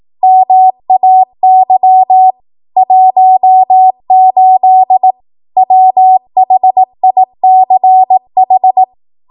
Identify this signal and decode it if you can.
Morse code, heard as a keyed tone, about 18 wpm, 750 Hz. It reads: MAY 18 WHICH
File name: Morse / 18WPM